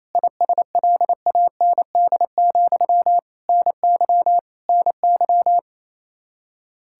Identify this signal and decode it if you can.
Morse code, heard as a keyed tone, about 27 wpm, 700 Hz. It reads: ISLAND, NY NY
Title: Practice QSO - Exchange 1 - Beginning - 28wpm